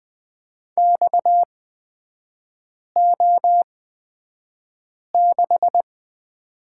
Morse code, heard as a keyed tone, about 20 wpm, spaced as for 5 wpm, 700 Hz. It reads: XO6
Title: 250 Lesson 40e - BK - Three Characters 20wpm